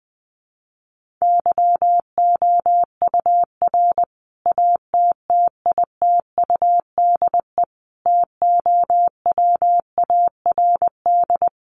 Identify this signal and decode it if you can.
Morse code, heard as a keyed tone, about 20 wpm, 700 Hz. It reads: YOUR ATTITUDE TOWARD
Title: Sentences from Top 1300 Words 20wpm